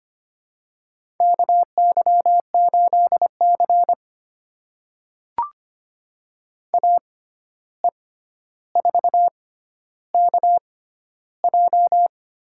Morse code, heard as a keyed tone, about 25 wpm, spaced as for 12 wpm, 700 Hz. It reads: KY8C AE4KJ
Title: USA Call Signs - 6x Character Spacing - 25wpm